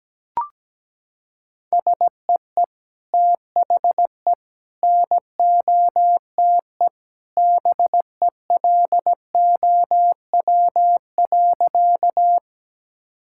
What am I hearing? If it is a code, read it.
Morse code, heard as a keyed tone, about 17 wpm, 700 Hz. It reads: SEE THE NOTE BELOW.
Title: Sentences from Top 1000 Words - Review Part 5 - 17wpm